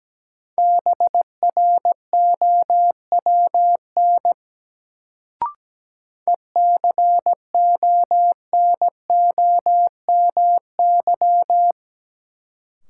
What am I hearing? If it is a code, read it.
Morse code, heard as a keyed tone, about 17 wpm, 700 Hz. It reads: BROWN ECONOMY
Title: Sentences from Top 1200 Words - New Word Drill Alt - 17wpm